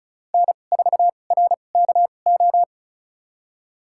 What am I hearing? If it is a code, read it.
Morse code, heard as a keyed tone, about 36 wpm, spaced as for 25 wpm, 700 Hz. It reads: N4RKO
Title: USA Call Signs - 2x Character Spacing - 35wpm